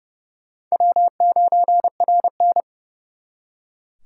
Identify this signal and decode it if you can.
Morse code, heard as a keyed tone, about 38 wpm, 700 Hz. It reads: W9RN